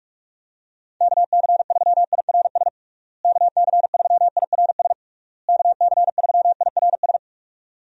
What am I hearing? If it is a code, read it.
Morse code, heard as a keyed tone, about 42 wpm, 700 Hz. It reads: KC3IRS KC3IRS KC3IRS
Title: USA Call Signs - 6 Characters - Repeated 3x - 45wpm